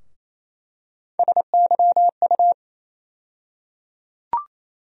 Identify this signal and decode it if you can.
Morse code, heard as a keyed tone, about 28 wpm, 700 Hz. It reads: SYU
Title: USA Call Sign Postfixes - 3 Letters 28wpm